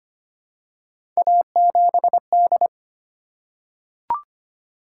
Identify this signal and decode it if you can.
Morse code, heard as a keyed tone, about 25 wpm, 700 Hz. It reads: A7D
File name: Three Letter-Number Combinations - 25wpm